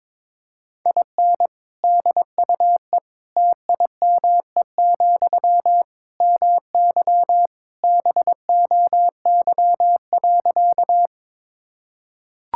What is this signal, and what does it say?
Morse code, heard as a keyed tone, about 22 wpm, 700 Hz. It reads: IN DUE TIME, MY BOY.
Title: Sentences from Top 1000 Words - Review Part 6 - 22wpm